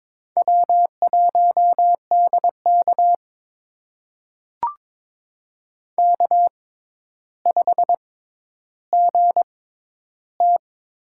Morse code, heard as a keyed tone, about 21 wpm, spaced as for 11 wpm, 700 Hz. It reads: W1DK K5GT
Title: USA Call Signs - 6x Character Spacing - 22wpm